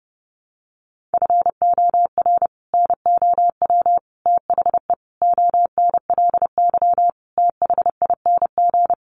Morse code, heard as a keyed tone, about 28 wpm, 700 Hz. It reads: FOR NOW THE ONLY THING
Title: Sentences from Top 200 Words 30wpm